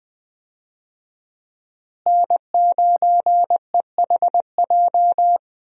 Morse code, heard as a keyed tone, about 20 wpm, 700 Hz. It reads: N9EHJ
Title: USA Call Signs - 5 Characters - Repeated 2x - 20wpm